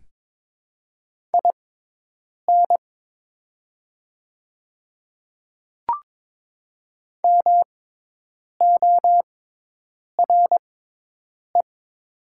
Morse code, heard as a keyed tone, about 22 wpm, spaced as for 8 wpm, 700 Hz. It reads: IN MORE
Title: 6x Character Spacing - Top 200 Words - 22wpm